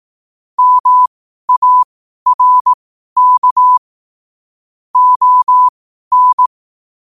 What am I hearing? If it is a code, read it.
Morse code, heard as a keyed tone, about 18 wpm, spaced as for 12 wpm, 1000 Hz. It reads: MARK ON
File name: qotdfiles / 12wpm